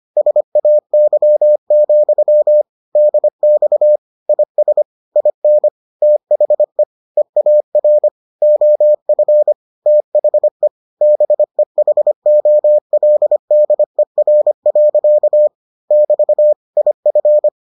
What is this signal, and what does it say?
Morse code, heard as a keyed tone, about 25 wpm, 600 Hz. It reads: SAY, DX IS IN THE EAR OF THE BEHOLDER. <BT> IF